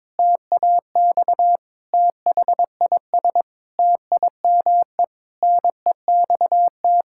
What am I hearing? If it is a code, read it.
Morse code, heard as a keyed tone, about 22 wpm, 700 Hz. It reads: TAX THIS TIME NEXT
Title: Sentences from Top 1000 Words - Review Part 5 - 22wpm